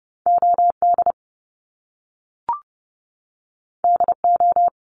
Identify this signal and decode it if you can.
Morse code, heard as a keyed tone, about 30 wpm, 700 Hz. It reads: OD DO
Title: Top Two Letter Combinations in Words 30wpm